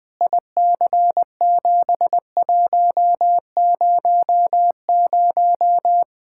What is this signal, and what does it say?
Morse code, heard as a keyed tone, about 20 wpm, 700 Hz. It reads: IC7100
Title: Practice QSO - Exchange 2 - Info - 20wpm